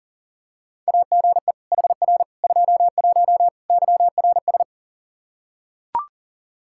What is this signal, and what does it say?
Morse code, heard as a keyed tone, about 37 wpm, 700 Hz. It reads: AGE HR 21 YRS
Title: K1USN SST QSO Practice 40wpm